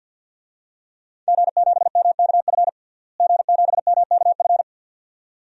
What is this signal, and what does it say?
Morse code, heard as a keyed tone, about 52 wpm, 700 Hz. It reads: G7MKF G7MKF
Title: DXCC-100 Call Signs - 5 Characters - Repeated 2x - 50wpm